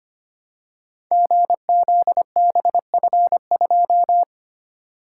Morse code, heard as a keyed tone, about 25 wpm, 700 Hz. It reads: GZBF2
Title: Code Groups - 5 Letter:Numbers - 25wpm